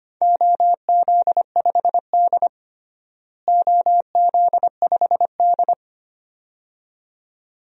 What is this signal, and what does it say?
Morse code, heard as a keyed tone, about 25 wpm, 700 Hz. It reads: OZ5D OZ5D
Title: DXCC-100 Call Signs - 4 Characters - Repeated 2x - 25wpm